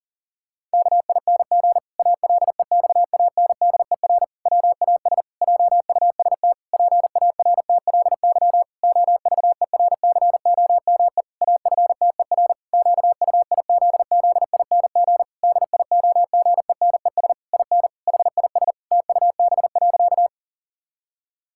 Morse code, heard as a keyed tone, about 40 wpm, 700 Hz. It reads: KING ALEXANDER WAS JUST PARTLY OVERCOME AFTER QUIZZING DIOGENES IN HIS TUB.
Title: Pangram Challenge v2 - 40wpm